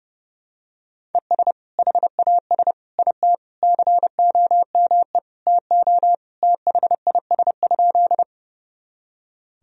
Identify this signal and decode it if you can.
Morse code, heard as a keyed tone, about 32 wpm, 700 Hz. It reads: ES HAS IT COME TO THIS?
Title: Sentences from Top 100 Words and Abbreviations 30wpm